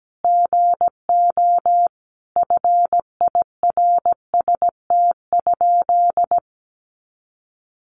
Morse code, heard as a keyed tone, about 17 wpm, 700 Hz. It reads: GO FIRST?
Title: Sentences from Top 100 Words 17wpm